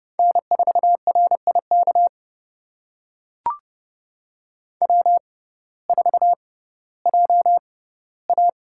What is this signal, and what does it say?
Morse code, heard as a keyed tone, about 30 wpm, spaced as for 17 wpm, 700 Hz. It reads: N4RIK W4JA
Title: USA Call Signs - 6x Character Spacing - 30wpm